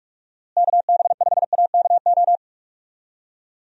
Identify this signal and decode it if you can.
Morse code, heard as a keyed tone, about 44 wpm, 700 Hz. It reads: KB5AKO
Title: USA Call Signs - 6 Characters - Repeated 2x - 45wpm